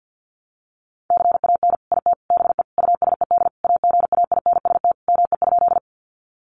Morse code, heard as a keyed tone, about 50 wpm, 700 Hz. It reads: CAN IT BE USED AGAINST ME?
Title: Sentences from Top 600 Words 50wpm